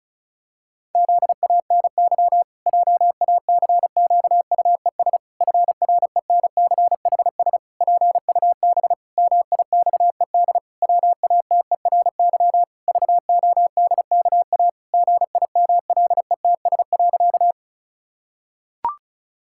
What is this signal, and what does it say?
Morse code, heard as a keyed tone, about 36 wpm, 700 Hz. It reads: ZANY JACQUES FRENCHS PUB MIXED WATERY VODKA GIMLETS.
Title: Pangram Challenge v2 - 35wpm